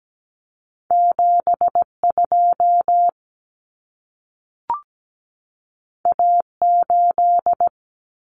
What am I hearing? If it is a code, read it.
Morse code, heard as a keyed tone, about 17 wpm, 700 Hz. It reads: 72 A8
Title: Common Cut Numbers - 2-Digit 17wpm